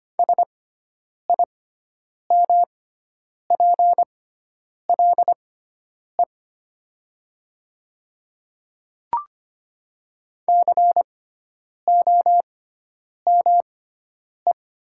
Morse code, heard as a keyed tone, about 25 wpm, spaced as for 9 wpm, 700 Hz. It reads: SIMPLE COME
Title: 6x Character Spacing - Top 500 Words - 25wpm